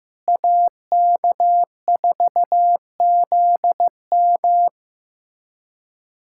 Morse code, heard as a keyed tone, about 15 wpm, 700 Hz. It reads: AK4ZM
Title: USA Callsign DE Callsign - 5 Characters - 15wpm